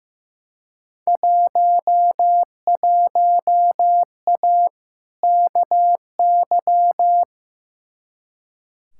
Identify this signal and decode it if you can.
Morse code, heard as a keyed tone, about 15 wpm, 700 Hz. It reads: 11A KY
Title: Field Day Exchanges Encore - 15wpm